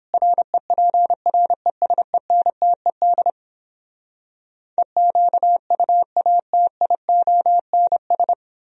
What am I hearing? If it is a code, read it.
Morse code, heard as a keyed tone, about 29 wpm, 700 Hz. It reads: REPRESENTED EQUATIONS